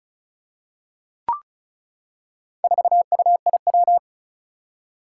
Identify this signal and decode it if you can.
Morse code, heard as a keyed tone, about 36 wpm, 700 Hz. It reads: 4UIW
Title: Four Challenging Letter-Number Combinations 35wpm